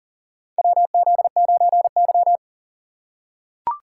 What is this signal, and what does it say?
Morse code, heard as a keyed tone, about 40 wpm, 700 Hz. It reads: WZ9Y
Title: USA Call Signs - 4 Characters - 2x Character Spacing - 40wpm